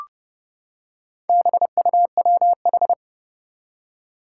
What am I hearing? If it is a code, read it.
Morse code, heard as a keyed tone, about 30 wpm, 700 Hz. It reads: BUWH
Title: Four Challenging Letter-Number Combinations 30wpm